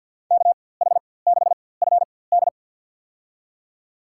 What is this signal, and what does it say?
Morse code, heard as a keyed tone, about 53 wpm, spaced as for 24 wpm, 700 Hz. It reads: KH6FD
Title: USA Call Signs - 4x Character Spacing - 50wpm